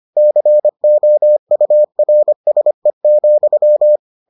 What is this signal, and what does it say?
Morse code, heard as a keyed tone, about 25 wpm, 600 Hz. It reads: COURSE,